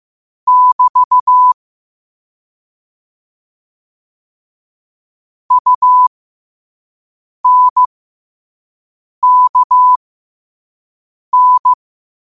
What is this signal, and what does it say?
Morse code, heard as a keyed tone, about 15 wpm, spaced as for 5 wpm, 1000 Hz. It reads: <BT> UNKN
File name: qotdfiles / 5wpm